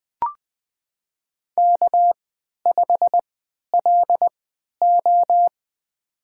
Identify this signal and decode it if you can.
Morse code, heard as a keyed tone, about 20 wpm, spaced as for 11 wpm, 700 Hz. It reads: K5LO
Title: USA Call Signs - 4 Characters - 3x Character Spacing - 20wpm